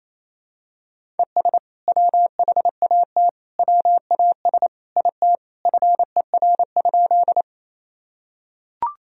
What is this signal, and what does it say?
Morse code, heard as a keyed tone, about 28 wpm, 700 Hz. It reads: ES WHAT WAS IT FER?
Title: Sentences from Top 100 Words and Abbreviations 28wpm